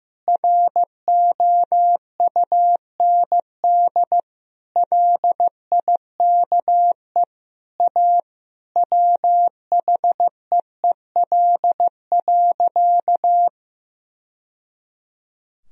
Morse code, heard as a keyed tone, about 15 wpm, 700 Hz. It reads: ROUND LIKE A WHEEL.
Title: Sentences from Top 1000 Words - Review Part 3 - 15wpm